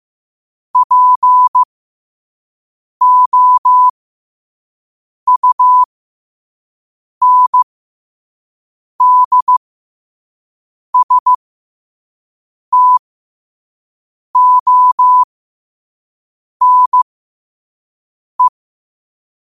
Morse code, heard as a keyed tone, about 15 wpm, spaced as for 5 wpm, 1000 Hz. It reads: POUNDSTONE
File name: qotdfiles / 5wpm